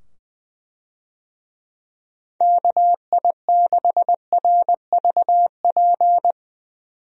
Morse code, heard as a keyed tone, about 20 wpm, 700 Hz. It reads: KI6RVP
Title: USA Call Signs - 6 Characters - 20wpm